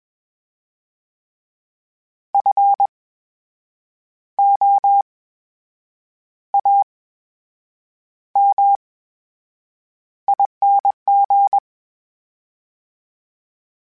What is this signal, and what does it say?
Morse code, heard as a keyed tone, about 21 wpm, spaced as for 7 wpm, 800 Hz. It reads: FOAMING
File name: Special INGs-mp3s / Special INGs-mp3s / VerySlowToFaster Ing @20